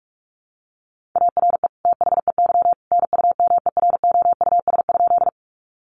Morse code, heard as a keyed tone, about 44 wpm, 700 Hz. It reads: ARE THEY NUMEROUS?
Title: Sentences from Top 1400 Words 45wpm